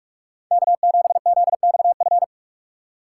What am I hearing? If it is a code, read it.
Morse code, heard as a keyed tone, about 46 wpm, 700 Hz. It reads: K7ZXF